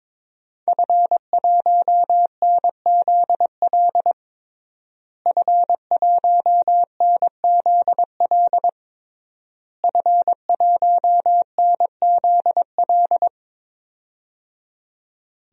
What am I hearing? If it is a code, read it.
Morse code, heard as a keyed tone, about 22 wpm, 700 Hz. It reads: F1NZL F1NZL F1NZL
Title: DXCC-100 Call Signs - 5 Characters - Repeated 3x - 22wpm